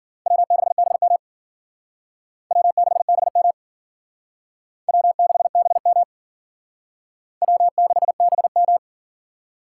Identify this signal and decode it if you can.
Morse code, heard as a keyed tone, about 52 wpm, 700 Hz. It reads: W6BM W6BM W6BM W6BM